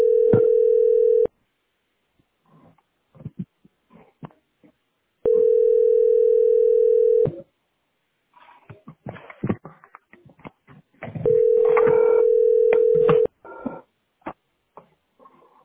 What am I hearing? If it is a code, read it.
Ringback tone (North America), 440 + 480 Hz.